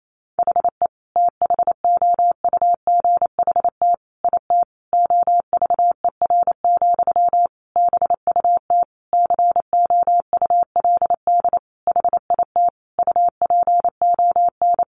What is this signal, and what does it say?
Morse code, heard as a keyed tone, about 28 wpm, 700 Hz. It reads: HE THOUGHT IT OVER, BUT COULD HIT UPON
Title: Sentences from Top 1200 Words 28wpm